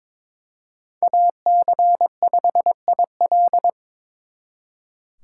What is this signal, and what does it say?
Morse code, heard as a keyed tone, about 26 wpm, 700 Hz. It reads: AC5IL
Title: Speed Racing - USA 2x2 Call Signs 22wpm